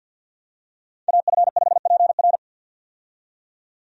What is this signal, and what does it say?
Morse code, heard as a keyed tone, about 52 wpm, 700 Hz. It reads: AF5PR
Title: USA Call Signs - 5 Characters - 50wpm